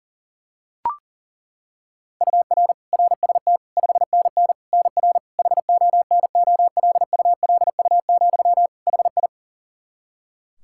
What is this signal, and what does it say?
Morse code, heard as a keyed tone, about 40 wpm, 700 Hz. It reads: UR RST 5NN NR HONOLULU, HI
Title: Practice QSO - Exchange 1 - Beginning - 40wpm